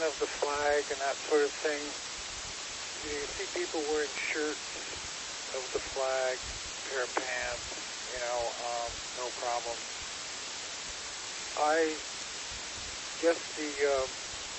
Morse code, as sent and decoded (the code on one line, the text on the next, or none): none